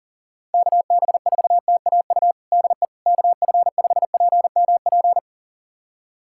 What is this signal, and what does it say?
Morse code, heard as a keyed tone, about 40 wpm, 700 Hz. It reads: KB4TAU DE KF5PMP
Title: USA Callsign DE Callsign - 6 Characters - 40wpm